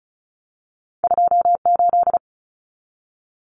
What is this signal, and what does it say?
Morse code, heard as a keyed tone, about 34 wpm, 700 Hz. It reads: 28
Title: Numbers - 2-Digit 35wpm